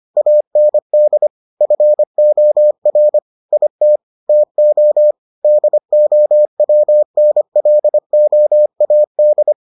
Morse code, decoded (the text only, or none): AND FOR IT TO DOWNLOAD